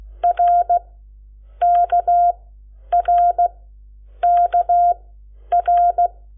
Morse code, heard as a keyed tone, about 16 wpm, spaced as for 9 wpm, 700 Hz. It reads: RKRKR
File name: Lecciones / CLASE Nº7